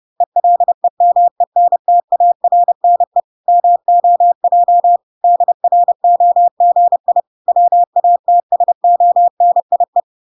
Morse code, heard as a keyed tone, about 30 wpm, 700 Hz. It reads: ELEMENTARNE MOJ DROGI WATSONIE